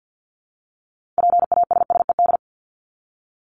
Morse code, heard as a keyed tone, about 50 wpm, 700 Hz. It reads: PASSED
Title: ED Suffix - Top 100 Words with Base Words 50wpm